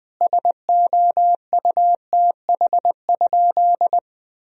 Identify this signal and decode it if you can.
Morse code, heard as a keyed tone, about 20 wpm, 700 Hz. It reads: SOUTH?
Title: Sentences from Top 1000 Words - Review Part 2 - 20wpm